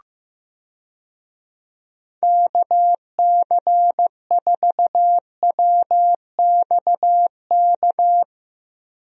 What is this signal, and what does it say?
Morse code, heard as a keyed tone, about 15 wpm, 700 Hz. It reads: KC4WXK
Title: USA Call Signs - 6 Characters - Repeated 3x - 15wpm